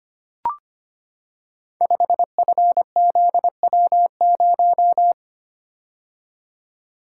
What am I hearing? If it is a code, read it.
Morse code, heard as a keyed tone, about 25 wpm, 700 Hz. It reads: 5FZW0
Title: Code Groups - 5 Letter:Numbers - 25wpm